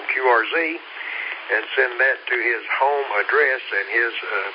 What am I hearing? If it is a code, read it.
Emergency Alert System (SAME) tones, heard as harsh buzzing bursts, with speech. It is heard from 0.9 to 4.2 s.